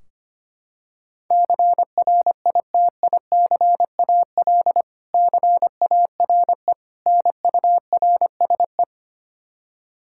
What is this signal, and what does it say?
Morse code, heard as a keyed tone, about 25 wpm, 700 Hz. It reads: CRITICAL CARE NURSE
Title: K1USN SST QSO Practice 25wpm